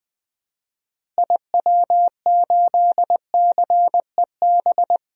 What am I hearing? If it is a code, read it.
Morse code, heard as a keyed tone, about 20 wpm, 700 Hz. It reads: IW8CEB
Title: DXCC-100 Call Signs - 6 Characters - Repeated 2x - 20wpm